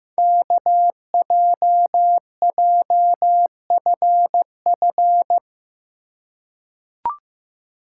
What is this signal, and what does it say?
Morse code, heard as a keyed tone, about 15 wpm, 700 Hz. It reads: KJJFF
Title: Code Groups - 5 Letters - 15wpm